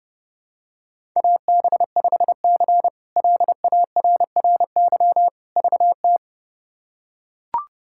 Morse code, heard as a keyed tone, about 30 wpm, 700 Hz. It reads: AB5C LARRY VT
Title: K1USN SST QSO Practice - 30wpm